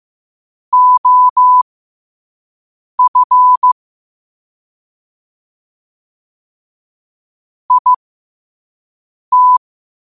Morse code, heard as a keyed tone, about 15 wpm, spaced as for 5 wpm, 1000 Hz. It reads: OF IT
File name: qotdfiles / 5wpm